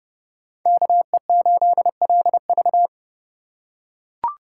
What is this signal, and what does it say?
Morse code, heard as a keyed tone, about 30 wpm, 700 Hz. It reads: KE8LV
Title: USA Call Signs - 5 Characters - 6x Character Spacing - 30wpm